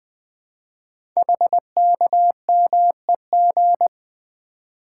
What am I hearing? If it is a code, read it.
Morse code, heard as a keyed tone, about 20 wpm, 700 Hz. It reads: HKMEG
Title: Code Groups - 5 Letters - 20wpm